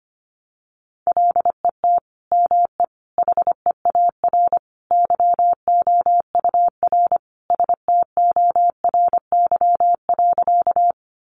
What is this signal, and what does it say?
Morse code, heard as a keyed tone, about 25 wpm, 700 Hz. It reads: LET ME HEAR YOUR STORY.
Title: Sentences from Top 300 Words 25wpm